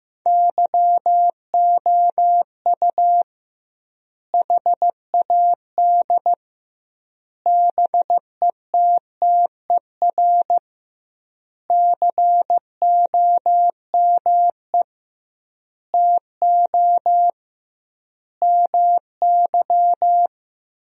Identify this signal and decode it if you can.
Morse code, heard as a keyed tone, about 15 wpm, 700 Hz. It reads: YOU HAD BETTER COME TO MY
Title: 2x Word-Spacing - Sentences from Top 400 Words - 15wpm